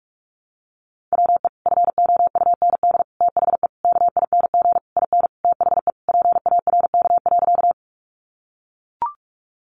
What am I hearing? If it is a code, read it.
Morse code, heard as a keyed tone, about 44 wpm, 700 Hz. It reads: WE FOUND THE KING IN THE PARK.
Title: Sentences from Top 1300 Words 45wpm